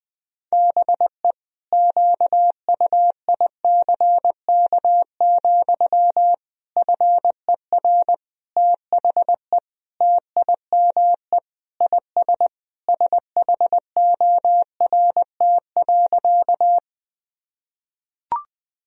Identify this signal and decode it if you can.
Morse code, heard as a keyed tone, about 20 wpm, 700 Hz. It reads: BE QUICK, FER THE TIME IS SHORT.
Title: Sentences from Top 500 Words and Abbreviations 20wpm